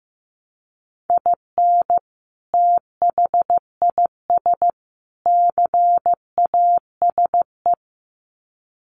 Morse code, heard as a keyed tone, about 15 wpm, 700 Hz. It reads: IN THIS CASE
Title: Sets of 3 Words 15wpm